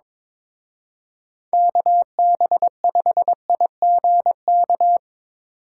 Morse code, heard as a keyed tone, about 25 wpm, 700 Hz. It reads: KB5IGK
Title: Speed Racing - USA 2x3 Call Signs 22wpm